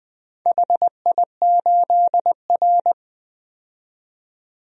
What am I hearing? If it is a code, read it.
Morse code, heard as a keyed tone, about 20 wpm, 700 Hz. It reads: HI8R
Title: DXCC-100 Call Signs - 4 Characters - 20wpm